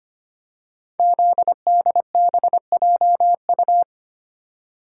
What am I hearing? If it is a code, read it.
Morse code, heard as a keyed tone, about 25 wpm, 700 Hz. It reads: ZDBJU
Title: Code Groups - 5 Letters - 25wpm